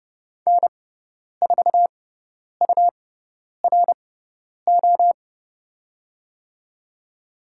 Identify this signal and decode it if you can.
Morse code, heard as a keyed tone, about 31 wpm, spaced as for 10 wpm, 700 Hz. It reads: N4URO
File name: Lesson 17g - Letter u - Call Signs 30wpm